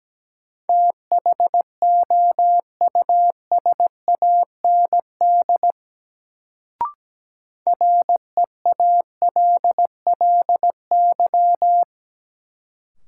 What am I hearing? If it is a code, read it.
Morse code, heard as a keyed tone, about 17 wpm, 700 Hz. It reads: THOUSAND REALLY
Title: Sentences from Top 600 Words - New Word Drill - 17wpm